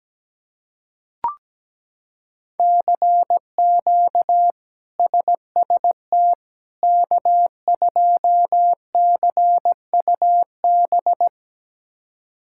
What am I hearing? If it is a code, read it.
Morse code, heard as a keyed tone, about 17 wpm, 700 Hz. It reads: CQ SST K2CUB
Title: K1USN SST QSO Practice - 17wpm